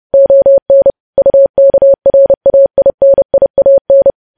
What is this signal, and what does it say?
Morse code, heard as a keyed tone, about 30 wpm, 550 Hz. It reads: ON UKRAINIAN
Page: news headlines in morse code at 30 wpm